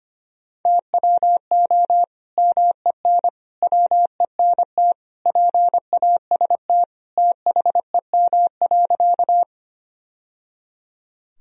Morse code, heard as a keyed tone, about 25 wpm, 700 Hz. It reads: TWO MEN WENT PAST THEM.
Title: Sentences from Top 1000 Words - Review Part 3 - 25wpm